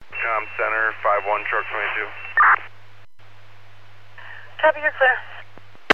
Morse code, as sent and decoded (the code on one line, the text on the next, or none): none